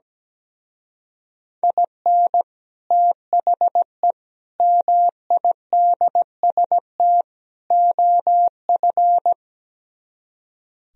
Morse code, heard as a keyed tone, about 17 wpm, 700 Hz. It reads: IN THE MIDST OF
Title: Speed-Racing - Sets of 4 Words 17wpm